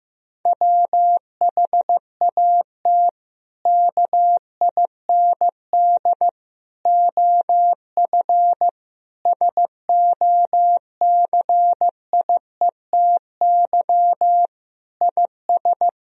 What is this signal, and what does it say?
Morse code, heard as a keyed tone, about 15 wpm, 700 Hz. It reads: WHAT KIND OF SOCIETY IS
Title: Sentences from Top 1000 Words - Review Part 4 - 15wpm